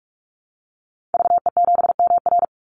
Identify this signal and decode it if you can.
Morse code, heard as a keyed tone, about 44 wpm, 700 Hz. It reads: VE7MR
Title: Call Signs Repeated 2x - 45wpm